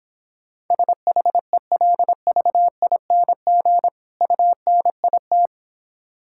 Morse code, heard as a keyed tone, about 26 wpm, 700 Hz. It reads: SHELVING UNIT